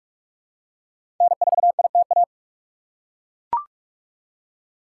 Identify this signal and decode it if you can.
Morse code, heard as a keyed tone, about 42 wpm, 700 Hz. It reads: N4ITA
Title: USA Call Signs - 5 Characters - 3x Character Spacing - 45wpm